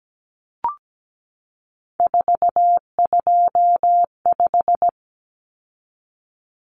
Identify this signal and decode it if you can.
Morse code, heard as a keyed tone, about 17 wpm, 700 Hz. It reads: 425
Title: Numbers - 3-Digit 17wpm